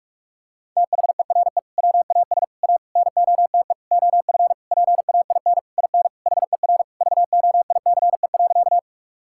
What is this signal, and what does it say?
Morse code, heard as a keyed tone, about 43 wpm, 700 Hz. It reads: THERE WAS A NOTE OF PAIN IN HER VOICE.
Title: Sentences from Top 1600 Words 45wpm